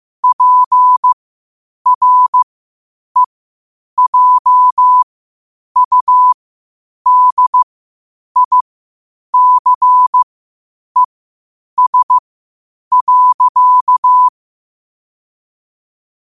Morse code, heard as a keyed tone, about 15 wpm, spaced as for 8 wpm, 1000 Hz. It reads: PREJUDICES.
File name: qotdfiles / 8wpm